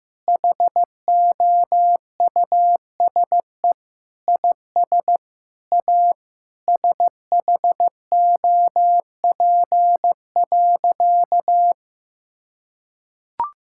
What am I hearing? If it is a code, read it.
Morse code, heard as a keyed tone, about 15 wpm, 700 Hz. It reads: HOUSE IS A SHOP.
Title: Sentences from Top 1600 Words 15wpm